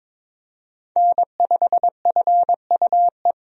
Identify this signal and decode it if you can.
Morse code, heard as a keyed tone, about 21 wpm, 700 Hz. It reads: N5FUE
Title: USA Call Signs - 5 Characters - Repeated 2x - 22wpm